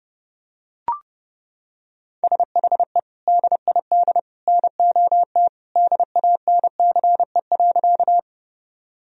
Morse code, heard as a keyed tone, about 30 wpm, 700 Hz. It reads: SHE DID NOT DANCE.
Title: Sentences from Top 1000 Words - Review Part 4 - 30wpm